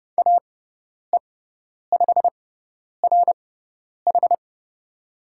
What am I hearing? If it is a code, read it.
Morse code, heard as a keyed tone, about 30 wpm, spaced as for 10 wpm, 700 Hz. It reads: AE5RH
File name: Lesson 18g - Letter m - Call Signs 30wpm